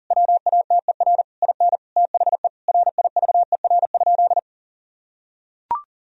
Morse code, heard as a keyed tone, about 40 wpm, 700 Hz. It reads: WATER IN THE RIVER?
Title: Sentences from Top 1000 Words - Review Part 4 - 40wpm